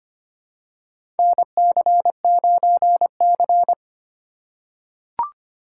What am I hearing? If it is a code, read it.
Morse code, heard as a keyed tone, about 25 wpm, 700 Hz. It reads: NC9C
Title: USA Call Signs - 4x Character Spacing - 25wpm